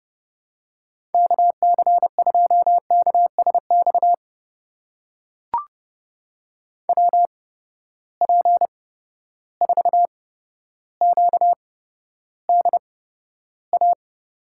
Morse code, heard as a keyed tone, about 32 wpm, spaced as for 13 wpm, 700 Hz. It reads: KC2KSX WP4QDA
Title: USA Call Signs - 8x Character Spacing - 30wpm